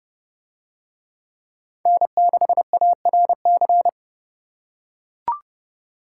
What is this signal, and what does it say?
Morse code, heard as a keyed tone, about 30 wpm, 700 Hz. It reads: N6ARC